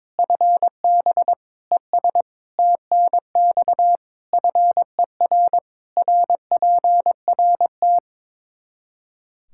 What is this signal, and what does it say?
Morse code, heard as a keyed tone, about 23 wpm, 700 Hz. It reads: FB ES TNX FER RPRT
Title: Practice QSO - Exchange 2 - Info - 22wpm